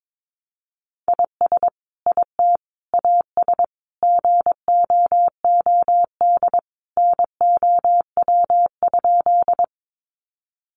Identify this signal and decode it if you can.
Morse code, heard as a keyed tone, about 22 wpm, 700 Hz. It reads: IS IT AS GOOD NOW?
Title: Sentences from Top 200 Words 22wpm